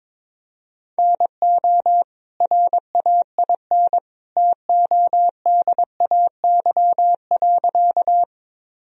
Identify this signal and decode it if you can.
Morse code, heard as a keyed tone, about 22 wpm, 700 Hz. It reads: NO RAIN TODAY.
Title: Sentences from Top 1000 Words - Review Part 1 - 22wpm